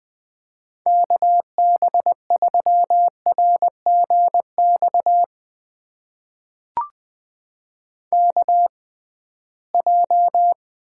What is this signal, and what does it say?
Morse code, heard as a keyed tone, about 21 wpm, spaced as for 15 wpm, 700 Hz. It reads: KB3RGX KJ
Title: USA Call Signs - 6 Characters - 6x Character Spacing - 20wpm